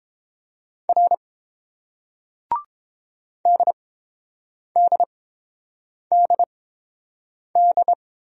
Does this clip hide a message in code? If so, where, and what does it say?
Yes: Morse code from 0.9 to 7.9 s, heard as a keyed tone, about 30 wpm, spaced as for 7 wpm, 700 Hz. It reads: R DDDD